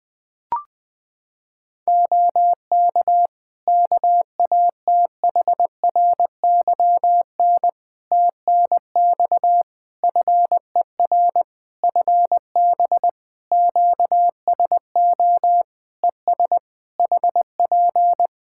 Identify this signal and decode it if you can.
Morse code, heard as a keyed tone, about 20 wpm, 700 Hz. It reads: OK KATHRYN TNX FER FB QSO ES HP
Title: K1USN SST QSO Practice 20wpm